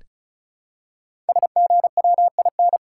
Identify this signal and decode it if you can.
Morse code, heard as a keyed tone, about 36 wpm, 700 Hz. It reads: SGWIN
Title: Code Groups - 5 Letters - 35wpm